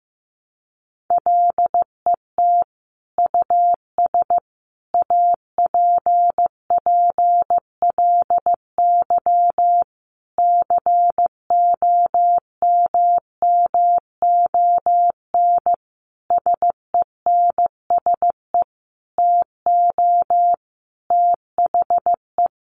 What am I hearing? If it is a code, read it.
Morse code, heard as a keyed tone, about 15 wpm, 700 Hz. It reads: LET US APPLY COMMON SENSE TO THE
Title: Sentences from Top 1200 Words 15wpm